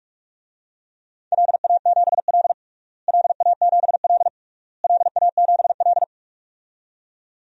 Morse code, heard as a keyed tone, about 47 wpm, 700 Hz. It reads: LA7L LA7L LA7L
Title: DXCC-100 Call Signs - 4 Characters - Repeated 3x - 45wpm